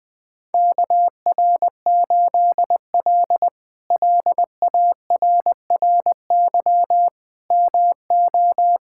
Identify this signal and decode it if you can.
Morse code, heard as a keyed tone, about 20 wpm, 700 Hz. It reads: KR8L LARRY MO
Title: K1USN SST QSO Practice - 20wpm